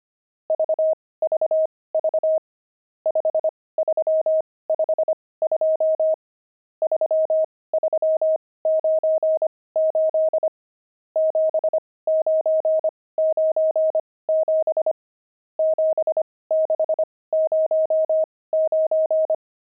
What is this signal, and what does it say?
Morse code, heard as a keyed tone, about 25 wpm, spaced as for 18 wpm, 630 Hz. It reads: VVV 5352 3398 7997 7609
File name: pangram-fs-18-cs-25-nums5